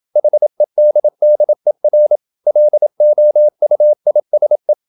Morse code, heard as a keyed tone, about 27 wpm, 600 Hz. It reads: HEDDER LOUISE
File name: Ferie QRQ 27wpm